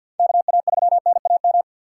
Morse code, heard as a keyed tone, about 53 wpm, 700 Hz. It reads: KA3NAM
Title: K1USN SST QSO Practice - 50wpm